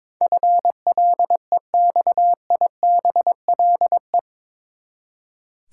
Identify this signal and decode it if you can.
Morse code, heard as a keyed tone, about 22 wpm, 700 Hz. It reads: FLEXIBLE
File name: practice / 22wpm / Eight Letter Words - 22wpm